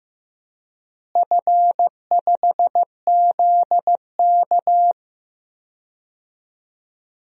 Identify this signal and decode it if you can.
Morse code, heard as a keyed tone, about 15 wpm, 700 Hz. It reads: F5ZK
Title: DXCC-100 Call Signs - 4 Characters - Repeated 2x - 15wpm